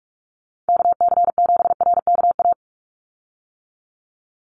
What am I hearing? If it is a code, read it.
Morse code, heard as a keyed tone, about 44 wpm, 700 Hz. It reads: KC7RKA
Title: USA Call Signs 45wpm